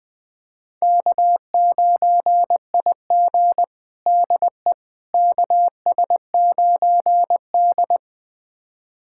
Morse code, heard as a keyed tone, about 21 wpm, 700 Hz. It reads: K9IG DE KS9D
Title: USA Callsign DE Callsign - 4 Characters - 20wpm